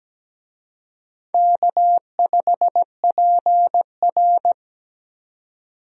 Morse code, heard as a keyed tone, about 17 wpm, 700 Hz. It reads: K5PR
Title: USA Call Signs - 4 Characters - 17wpm